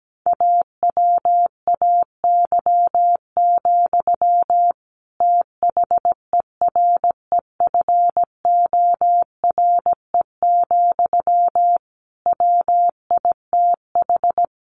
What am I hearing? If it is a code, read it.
Morse code, heard as a keyed tone, about 17 wpm, 700 Hz. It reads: AWAY, THEREFORE, WITH
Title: Sentences from Top 1500 Words 17wpm